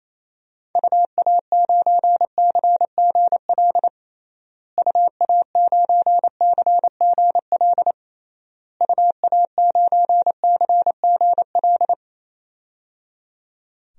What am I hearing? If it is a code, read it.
Morse code, heard as a keyed tone, about 27 wpm, 700 Hz. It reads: UA9CGL UA9CGL UA9CGL
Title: DXCC-100 Call Signs - 6 Characters - Repeated 3x - 28wpm